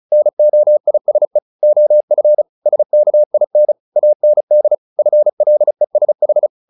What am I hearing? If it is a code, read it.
Morse code, heard as a keyed tone, about 34 wpm, 600 Hz. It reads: NOISE OF SKIN AND FLESH
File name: the old man and the sea 2 35wpm